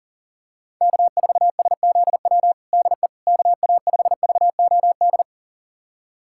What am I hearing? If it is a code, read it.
Morse code, heard as a keyed tone, about 40 wpm, 700 Hz. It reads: K4SZW DE KA5VOD
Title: USA Callsign DE Callsign 40wpm